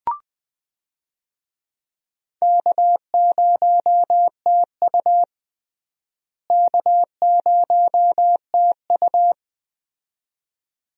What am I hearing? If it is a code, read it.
Morse code, heard as a keyed tone, about 20 wpm, 700 Hz. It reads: K0TU K0TU
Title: USA Call Signs - 4 Characters - Repeated 2x - 20wpm